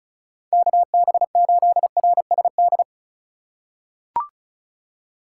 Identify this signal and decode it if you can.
Morse code, heard as a keyed tone, about 36 wpm, 700 Hz. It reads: KB8RSD
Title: USA Call Signs - 6 Characters - 3x Character Spacing - 35wpm